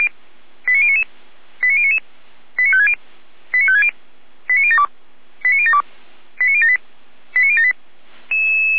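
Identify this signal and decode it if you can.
Five-tone selective call (ZVEI), heard as quick runs of five tones, tones of about 70 ms. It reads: 89099 89099 89560 89560 89072 89072 89078 89078
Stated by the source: ZVEI five-tone selective call received with an SDR.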